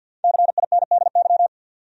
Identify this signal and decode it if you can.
Morse code, heard as a keyed tone, about 52 wpm, 700 Hz. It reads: CINDY
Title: USA Common Names - Max 5 Letters - 50wpm